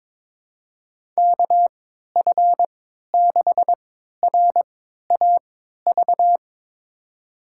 Morse code, heard as a keyed tone, about 23 wpm, spaced as for 13 wpm, 700 Hz. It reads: KF6RAV
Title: USA Call Signs - 6 Characters - 3x Character Spacing - 22wpm